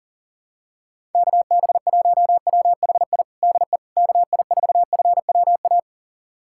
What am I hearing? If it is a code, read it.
Morse code, heard as a keyed tone, about 37 wpm, 700 Hz. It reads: KB1WHI DE KI4FWA
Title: USA Callsign DE Callsign - 6 Characters - 40wpm